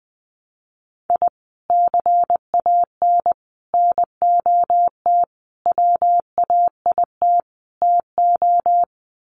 Morse code, heard as a keyed tone, about 20 wpm, 700 Hz. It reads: I CAN NOT WAIT TO
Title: Sets of 5 Words Encore 20wpm